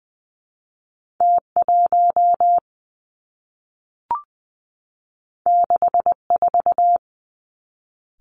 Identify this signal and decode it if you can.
Morse code, heard as a keyed tone, about 19 wpm, 700 Hz. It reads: T 1 6 4
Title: Two Challenging Letter-Number Combinations 20wpm